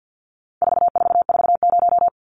Morse code, heard as a keyed tone, about 50 wpm, 700 Hz. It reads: 4440
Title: Numbers - 4-Digit 50wpm